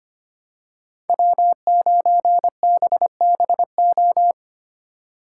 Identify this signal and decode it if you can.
Morse code, heard as a keyed tone, about 25 wpm, 700 Hz. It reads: W9BBO
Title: USA Call Signs - 3x Character Spacing - 25wpm